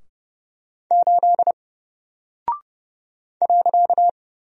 Morse code, heard as a keyed tone, about 29 wpm, 700 Hz. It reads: 8 .
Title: Single Character (Letter:Number:Punctuation:Prosign) 30wpm